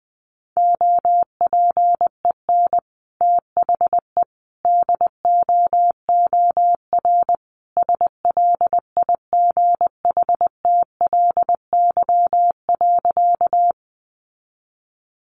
Morse code, heard as a keyed tone, about 20 wpm, 700 Hz. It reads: OPEN THE DOOR SLIGHTLY.
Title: Sentences from Top 1300 Words 20wpm